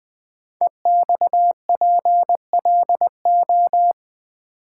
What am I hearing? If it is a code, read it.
Morse code, heard as a keyed tone, about 20 wpm, 700 Hz. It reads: EXPLO
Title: Build A Word - 5 Letters At A Time - 12-Letter Words - 20wpm